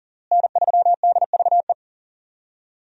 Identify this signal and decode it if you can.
Morse code, heard as a keyed tone, about 40 wpm, 700 Hz. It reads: N3DVE
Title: USA Call Signs - 5 Characters - Repeated 3x - 40wpm